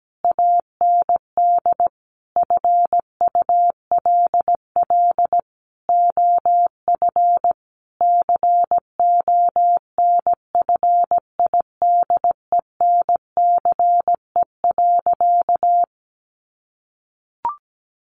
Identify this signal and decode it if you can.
Morse code, heard as a keyed tone, about 17 wpm, 700 Hz. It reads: AND FULL OF CONFIDENCE.
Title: Sentences from Top 1500 Words 17wpm